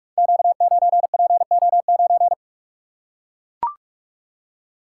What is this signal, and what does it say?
Morse code, heard as a keyed tone, about 44 wpm, 700 Hz. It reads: Q9PO9
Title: Code Groups - 5 Letter:Numbers - 45wpm